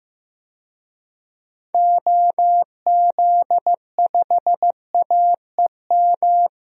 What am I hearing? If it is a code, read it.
Morse code, heard as a keyed tone, about 16 wpm, 700 Hz. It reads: OZ5AEM
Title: DXCC-100 Call Signs - 6 Characters - Repeated 3x - 15wpm